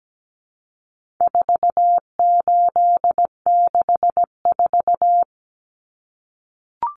Morse code, heard as a keyed tone, about 17 wpm, 700 Hz. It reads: 4864
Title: Numbers - 4-Digit 17wpm